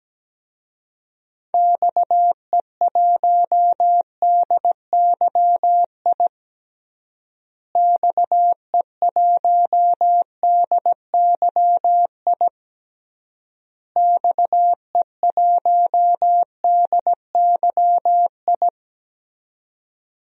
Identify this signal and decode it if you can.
Morse code, heard as a keyed tone, about 17 wpm, 700 Hz. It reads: XE1DYI XE1DYI XE1DYI
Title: DXCC-100 Call Signs - 6 Characters - Repeated 3x - 17wpm